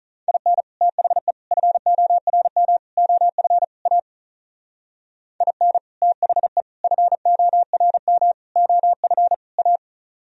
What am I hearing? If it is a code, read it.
Morse code, heard as a keyed tone, about 42 wpm, 700 Hz. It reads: IN THE FORM OF A IN THE FORM OF A
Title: Speed-Racing - Sets of 5 Words 35wpm